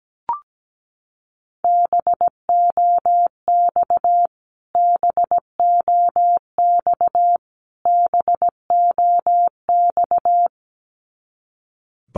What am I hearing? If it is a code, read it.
Morse code, heard as a keyed tone, about 17 wpm, 700 Hz. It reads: BOX BOX BOX
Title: Top 500 Words Repeated 17wpm